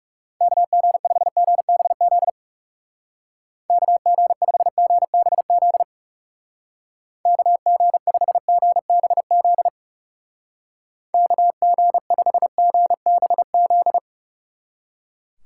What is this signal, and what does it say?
Morse code, heard as a keyed tone, about 40 wpm, 700 Hz. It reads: KG5GBZ KG5GBZ KG5GBZ KG5GBZ